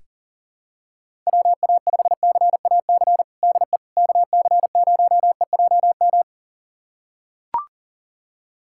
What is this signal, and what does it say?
Morse code, heard as a keyed tone, about 40 wpm, 700 Hz. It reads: WA5CAC DE KC0EJM
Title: USA Callsign DE Callsign 40wpm